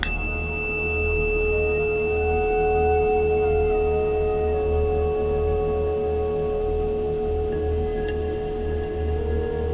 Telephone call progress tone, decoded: dial tone